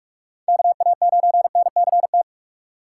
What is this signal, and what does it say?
Morse code, heard as a keyed tone, about 44 wpm, 700 Hz. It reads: KA9NCT
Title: USA Call Signs - 2x Character Spacing - 45wpm